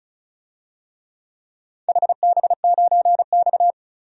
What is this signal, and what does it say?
Morse code, heard as a keyed tone, about 36 wpm, 700 Hz. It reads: HB9X
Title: DXCC-100 Call Signs - 4 Characters - 35wpm